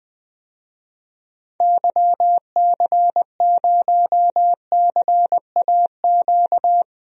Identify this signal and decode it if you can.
Morse code, heard as a keyed tone, about 20 wpm, 700 Hz. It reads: YC0CAQ
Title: DXCC-100 Call Signs - 6 Characters - Repeated 2x - 20wpm